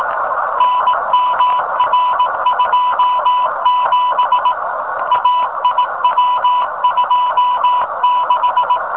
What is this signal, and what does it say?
Morse code, heard as a keyed tone, about 18 wpm, 1000 Hz, with noise. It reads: NMR27 AIW26
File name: rs15 telem